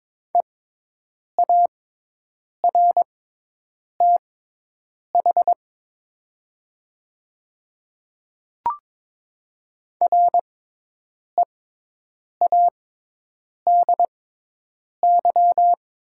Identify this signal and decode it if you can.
Morse code, heard as a keyed tone, about 22 wpm, spaced as for 8 wpm, 700 Hz. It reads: EARTH READY
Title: 6x Character Spacing - Top 500 Words - 22wpm